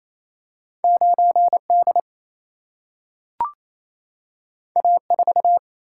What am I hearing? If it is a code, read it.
Morse code, heard as a keyed tone, about 28 wpm, 700 Hz. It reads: 9D A4
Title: Single-Letter Single-Number 28wpm